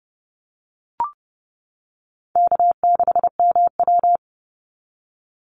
Morse code, heard as a keyed tone, about 30 wpm, 700 Hz. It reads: K6MW
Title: NA SOTA Chaser Callsigns 30wpm